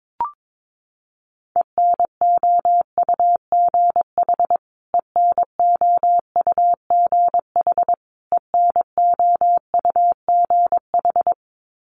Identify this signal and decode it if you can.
Morse code, heard as a keyed tone, about 22 wpm, 700 Hz. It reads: ENOUGH ENOUGH ENOUGH
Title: Top 500 Words Repeated 22wpm